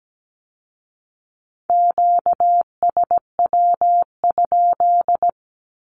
Morse code, heard as a keyed tone, about 17 wpm, 700 Hz. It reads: QSW?
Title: Q-Codes with Spoken Meaning - 17wpm